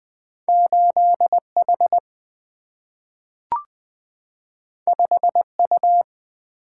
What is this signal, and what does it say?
Morse code, heard as a keyed tone, about 20 wpm, 700 Hz. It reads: 8 H 5 U
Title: Single-Letter Single-Number 20wpm